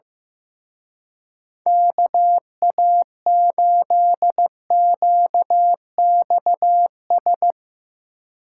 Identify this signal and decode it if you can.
Morse code, heard as a keyed tone, about 15 wpm, 700 Hz. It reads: KA8QXS
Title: Speed Racing - USA 2x3 Call Signs 15wpm